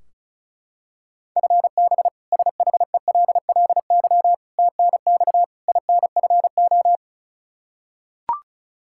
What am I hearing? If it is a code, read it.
Morse code, heard as a keyed tone, about 36 wpm, 700 Hz. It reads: FB SHELLY TNX INFO
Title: K1USN SST QSO Practice 35wpm